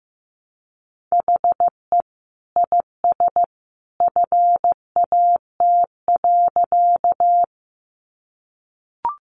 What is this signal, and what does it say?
Morse code, heard as a keyed tone, about 15 wpm, 700 Hz. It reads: HE IS FAT.
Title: Easy Phrases 15wpm